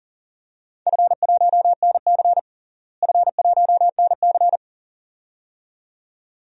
Morse code, heard as a keyed tone, about 37 wpm, 700 Hz. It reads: F1NC F1NC
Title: DXCC-100 Call Signs - 4 Characters - Repeated 2x - 40wpm